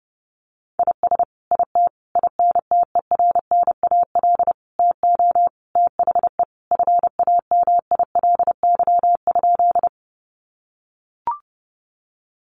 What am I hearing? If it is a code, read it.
Morse code, heard as a keyed tone, about 30 wpm, 700 Hz. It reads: IS IT INTERNAL TO THE FAMILY?
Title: Sentences from Top 1400 Words 30wpm